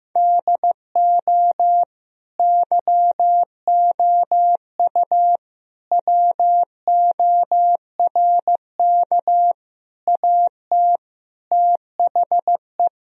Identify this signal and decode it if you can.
Morse code, heard as a keyed tone, about 15 wpm, 700 Hz. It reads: DO YOU WORK AT THE
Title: Sentences from Top 1000 Words - Review Part 1 - 15wpm